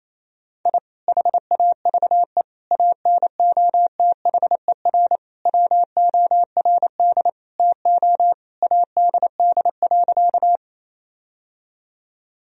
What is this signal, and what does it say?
Morse code, heard as a keyed tone, about 28 wpm, 700 Hz. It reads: I HAVE ANOTHER WORD TO ADD.
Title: Sentences from Top 1000 Words - Review Part 1 - 28wpm